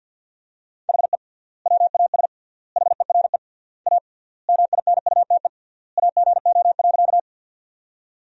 Morse code, heard as a keyed tone, about 50 wpm, 700 Hz. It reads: HE WAS HERE A MINUTE AGO.
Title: 3x Word-Spacing - Sentences from Top 500 Words - 50wpm